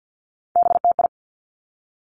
Morse code, heard as a keyed tone, about 50 wpm, 700 Hz. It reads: BTI
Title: Three Challenging Letter-Number Combinations 50wpm